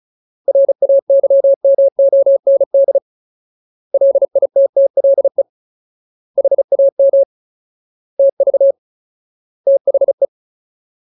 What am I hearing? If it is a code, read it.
Morse code, heard as a keyed tone, about 34 wpm, 550 Hz. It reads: RAYMOND LITTLE HAM TV THE
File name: Ultimate Word List / 35WPM Ultimate